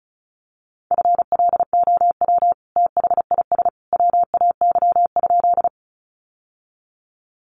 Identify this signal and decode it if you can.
Morse code, heard as a keyed tone, about 34 wpm, 700 Hz. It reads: FLOW THIS WAY?
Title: Sentences from Top 1200 Words 35wpm